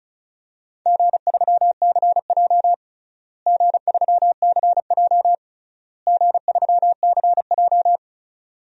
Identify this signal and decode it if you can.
Morse code, heard as a keyed tone, about 36 wpm, 700 Hz. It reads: G3CJ G3CJ G3CJ
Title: DXCC-100 Call Signs - 4 Characters - Repeated 3x - 35wpm